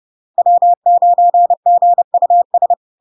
Morse code, heard as a keyed tone, about 30 wpm, 700 Hz. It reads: W9GUS
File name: LARK-FD-2022-Log-in-CW-30wpm-700-Hz